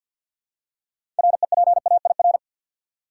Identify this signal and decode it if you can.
Morse code, heard as a keyed tone, about 52 wpm, 700 Hz. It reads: REPAIR
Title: Six Letter Words - 50wpm